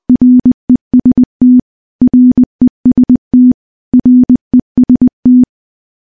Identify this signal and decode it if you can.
Morse code, heard as a keyed tone, about 20 wpm, 250 Hz. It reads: REST REST REST
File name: morse-training-20WPM